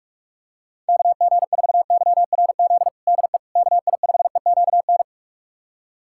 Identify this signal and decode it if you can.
Morse code, heard as a keyed tone, about 42 wpm, 700 Hz. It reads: KG4YRZ DE KI5EQN
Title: USA Callsign DE Callsign - 6 Characters - 45wpm